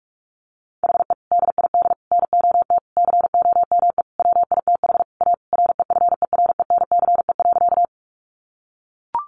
Morse code, heard as a keyed tone, about 44 wpm, 700 Hz. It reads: HE DID NOT COME WITH A REFERENCE.
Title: Sentences from Top 1200 Words 45wpm